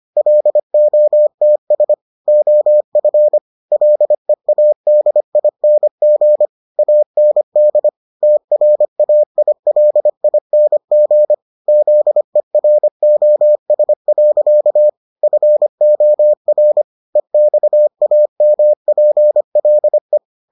Morse code, heard as a keyed tone, about 25 wpm, 600 Hz. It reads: LOTS OF LEADING AND TRAILING ZEROS. FOR EXAMPLE